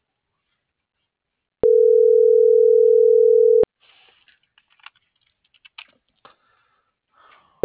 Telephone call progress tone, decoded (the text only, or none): ringback tone